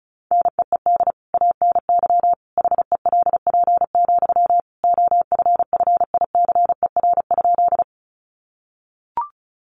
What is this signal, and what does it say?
Morse code, heard as a keyed tone, about 34 wpm, 700 Hz. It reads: NEED ANY HELP, OFFICER?
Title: Sentences from Top 1200 Words 35wpm